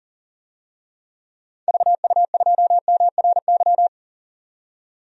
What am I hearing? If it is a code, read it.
Morse code, heard as a keyed tone, about 40 wpm, 700 Hz. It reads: VU2MRY
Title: DXCC-100 Call Signs - 6 Characters - 40wpm